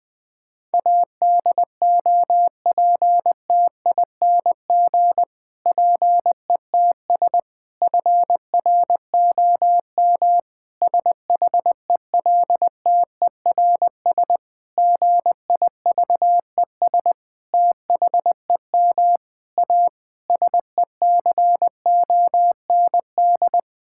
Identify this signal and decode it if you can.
Morse code, heard as a keyed tone, about 20 wpm, 700 Hz. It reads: ADOPTING PETS FROM SHELTERS GIVES THEM A SECOND
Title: Difficult Sentences with Hints - 20wpm